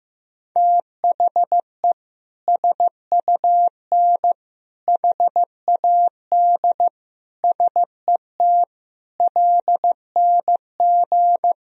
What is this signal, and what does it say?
Morse code, heard as a keyed tone, about 15 wpm, 700 Hz. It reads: THE SUN HAD SET LNG
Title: Sentences from Top 300 Words and Abbreviations 15wpm